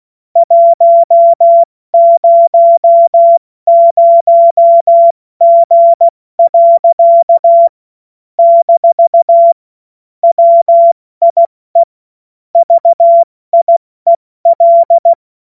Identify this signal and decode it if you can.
Morse code, heard as a keyed tone, about 16 wpm, 680 Hz. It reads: 100G. - WIE VIEL
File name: Morsekurs / Witze / Witz 07. Juni 2022 70 F80